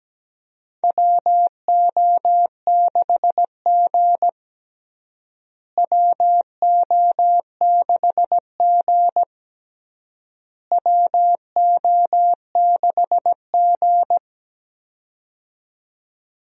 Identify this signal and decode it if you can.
Morse code, heard as a keyed tone, about 17 wpm, 700 Hz. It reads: WO6G WO6G WO6G
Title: USA Call Signs - 4 Characters - Repeated 3x - 17wpm